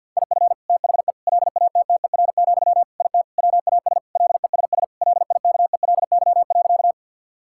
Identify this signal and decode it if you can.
Morse code, heard as a keyed tone, about 50 wpm, 700 Hz. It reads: IF THE LATTER, IT WAS LESS LIKELY.
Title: Sentences from Top 1600 Words 50wpm